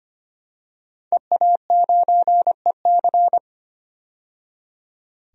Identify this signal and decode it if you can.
Morse code, heard as a keyed tone, about 25 wpm, 700 Hz. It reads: EA9EC
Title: DXCC-100 Call Signs - 5 Characters - 25wpm